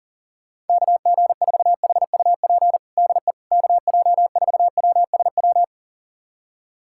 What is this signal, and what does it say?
Morse code, heard as a keyed tone, about 40 wpm, 700 Hz. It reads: KG4HUP DE KJ4WSW
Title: USA Callsign DE Callsign - 6 Characters - 40wpm